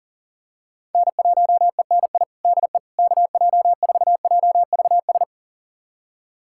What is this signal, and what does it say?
Morse code, heard as a keyed tone, about 40 wpm, 700 Hz. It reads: N1ENI DE KJ4JVS
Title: USA Callsign DE Callsign 40wpm